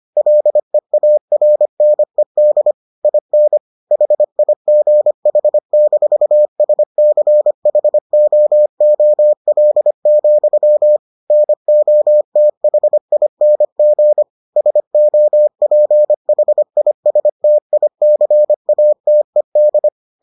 Morse code, heard as a keyed tone, about 25 wpm, 600 Hz. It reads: LEARNED IN HIGH-SCHOOL, NOTHING SOPHISTICATED